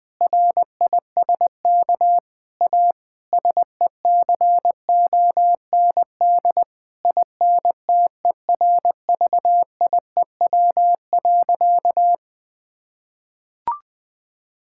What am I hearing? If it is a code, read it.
Morse code, heard as a keyed tone, about 20 wpm, 700 Hz. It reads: RISK A SECOND INTERVIEW.
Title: Sentences from Top 1600 Words 20wpm